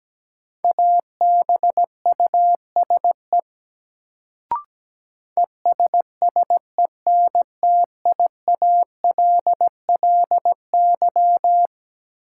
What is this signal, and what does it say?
Morse code, heard as a keyed tone, about 17 wpm, 700 Hz. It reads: ABUSE ESSENTIALLY
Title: Sentences from Top 1300 Words - New Word Drill - 17wpm